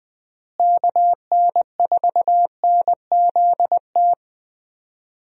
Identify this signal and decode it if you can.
Morse code, heard as a keyed tone, about 19 wpm, 700 Hz. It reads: KN4NZT
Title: USA Call Signs - 6 Characters - Repeated 3x - 20wpm